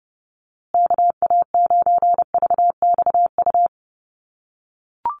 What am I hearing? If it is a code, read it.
Morse code, heard as a keyed tone, about 30 wpm, 700 Hz. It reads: KA9VXU
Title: USA Call Signs Repeated 2x 30wpm